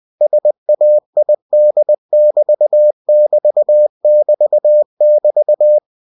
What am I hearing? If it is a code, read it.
Morse code, heard as a keyed tone, about 20 wpm, 600 Hz. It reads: SAID<BT><BT><BT><BT>
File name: cw-books / winnie-the-pooh / winnie-20wpm-page-0061